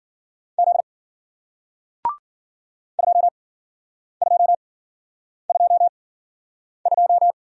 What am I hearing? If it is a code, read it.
Morse code, heard as a keyed tone, about 55 wpm, spaced as for 9 wpm, 700 Hz. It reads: Z 2222